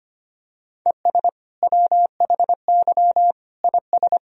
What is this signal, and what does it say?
Morse code, heard as a keyed tone, about 25 wpm, 700 Hz. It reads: ES WHY IS
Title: Sentences from Top 200 Words and Abbreviations 25wpm